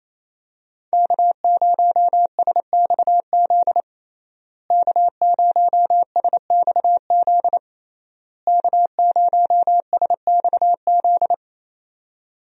Morse code, heard as a keyed tone, about 28 wpm, 700 Hz. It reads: K0SXZ K0SXZ K0SXZ
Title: USA Call Signs - 5 Characters - Repeated 3x - 28wpm